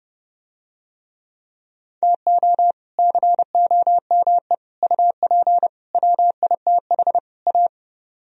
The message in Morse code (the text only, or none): TO COME UP WITH A